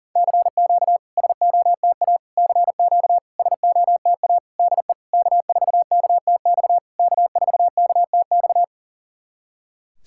Morse code, heard as a keyed tone, about 40 wpm, 700 Hz. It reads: CQ SOTA CQ SOTA DE K4KTX K4KTX
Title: SOTA QSO Practice 40wpm